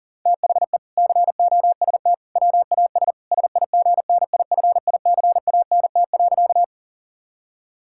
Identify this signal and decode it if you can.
Morse code, heard as a keyed tone, about 37 wpm, 700 Hz. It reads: THE COST WAS SIGNIFICANT.
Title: Sentences from Top 1000 Words - Review Part 6 - 40wpm